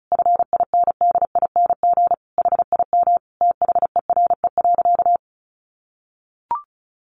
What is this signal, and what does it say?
Morse code, heard as a keyed tone, about 34 wpm, 700 Hz. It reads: FINDING HIM THERE.
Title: Sentences from Top 1100 Words 35wpm